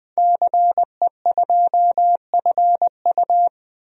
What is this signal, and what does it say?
Morse code, heard as a keyed tone, about 20 wpm, 700 Hz. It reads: CE2FU
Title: DXCC-100 Call Signs - 5 Characters - 20wpm